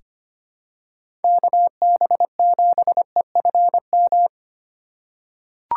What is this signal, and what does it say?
Morse code, heard as a keyed tone, about 25 wpm, 700 Hz. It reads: KB7EFM
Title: USA Call Signs - 3x Character Spacing - 25wpm